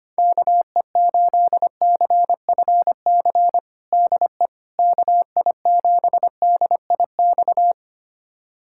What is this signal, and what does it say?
Morse code, heard as a keyed tone, about 25 wpm, 700 Hz. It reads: KE8CFC DE KI7DIX
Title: USA Callsign DE Callsign - 6 Characters - 25wpm